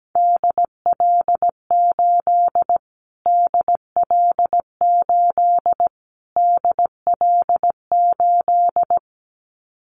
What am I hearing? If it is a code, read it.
Morse code, heard as a keyed tone, about 17 wpm, 700 Hz. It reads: DL8 DL8 DL8
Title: Callsign Prefixes for Top 25 DXCC - Repeated 3x - 17wpm